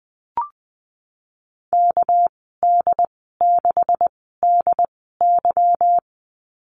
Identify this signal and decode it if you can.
Morse code, heard as a keyed tone, about 20 wpm, spaced as for 15 wpm, 700 Hz. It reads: KD6DY
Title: 207 Lesson 33g - Letter j - Call Signs 20wpm